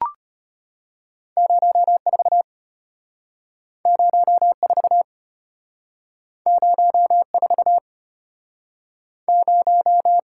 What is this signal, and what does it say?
Morse code, heard as a keyed tone, about 33 wpm, 700 Hz. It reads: 04 04 04 0